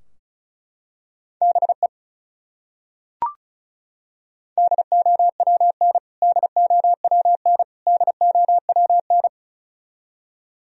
Morse code, heard as a keyed tone, about 36 wpm, 700 Hz. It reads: BE DOWN DOWN DOWN
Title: Top 100 Words Repeated 35wpm